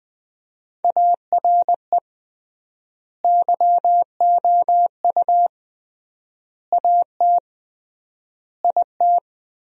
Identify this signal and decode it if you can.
Morse code, heard as a keyed tone, about 21 wpm, 700 Hz. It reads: ARE YOU AT IT
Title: 3x Word-Spacing - Sentences from Top 200 Words - 20wpm